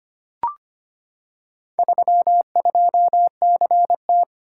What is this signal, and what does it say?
Morse code, heard as a keyed tone, about 25 wpm, 700 Hz. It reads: 32CT
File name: practice / 25wpm / Four Letter-Number Combinations - 25wpm